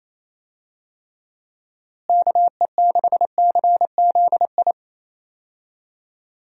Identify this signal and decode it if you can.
Morse code, heard as a keyed tone, about 28 wpm, 700 Hz. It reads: KE6CZI
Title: USA Call Signs - 6 Characters - Repeated 2x - 28wpm